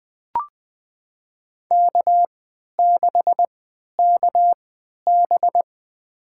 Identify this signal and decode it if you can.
Morse code, heard as a keyed tone, about 20 wpm, spaced as for 11 wpm, 700 Hz. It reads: K6KB
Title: USA Call Signs - 4 Characters - 3x Character Spacing - 20wpm